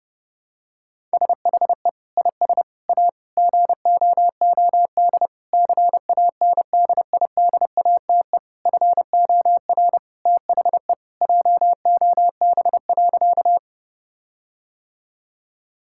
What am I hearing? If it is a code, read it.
Morse code, heard as a keyed tone, about 30 wpm, 700 Hz. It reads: SHE IS A GOOD CANDIDATE FOR THE JOB.
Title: Sentences from Top 1600 Words 30wpm